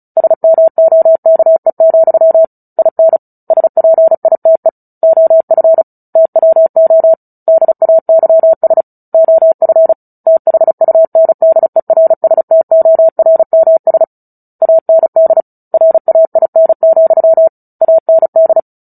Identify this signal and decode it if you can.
Morse code, heard as a keyed tone, about 34 wpm, 650 Hz. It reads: SMOKE, IN SPITE OF TWO DAYS OF THUNDERSTORMS AND RAIN, AND
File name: Book Two / 09-Chapter 9 - 35 WPM